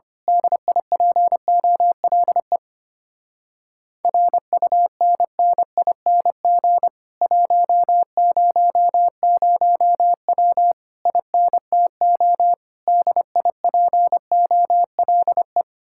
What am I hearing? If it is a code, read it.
Morse code, heard as a keyed tone, about 25 wpm, 700 Hz. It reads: DIPOLE RUNNING 100W INTO DIPOLE
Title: Speed-Racing Common QSO Phrases - 25wpm